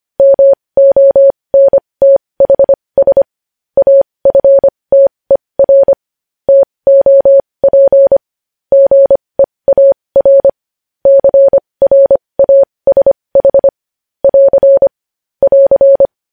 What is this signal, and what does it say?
Morse code, heard as a keyed tone, about 25 wpm, spaced as for 20 wpm, 550 Hz. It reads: MONTHS AFTER TOP GEAR CRASH <AR> <AR>
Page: News Headlines in Morse Code at 20 WPM